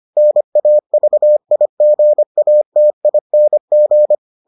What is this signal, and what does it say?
Morse code, heard as a keyed tone, about 25 wpm, 600 Hz. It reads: NAVIGATING